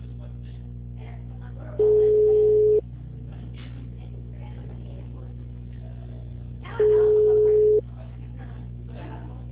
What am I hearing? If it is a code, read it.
Ringback tone (Europe, 425 Hz), with speech.